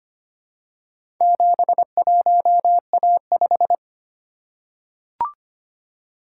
Morse code, heard as a keyed tone, about 25 wpm, 700 Hz. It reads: 71A5
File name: practice / 25wpm / Four Letter-Number Combinations - 25wpm